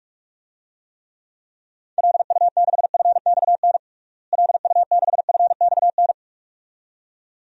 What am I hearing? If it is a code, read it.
Morse code, heard as a keyed tone, about 41 wpm, 700 Hz. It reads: LU6FXN LU6FXN
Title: DXCC-100 Call Signs - 6 Characters - Repeated 2x - 45wpm